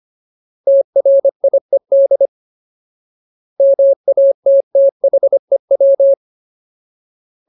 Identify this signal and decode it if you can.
Morse code, heard as a keyed tone, about 25 wpm, 550 Hz. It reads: TRIED MATTHEW
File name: Ultimate Word List / 25WPM Ultimate